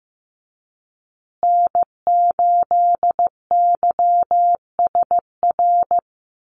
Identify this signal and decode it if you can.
Morse code, heard as a keyed tone, about 14 wpm, 700 Hz. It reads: N8YSR
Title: USA Call Signs Repeated 3x 15wpm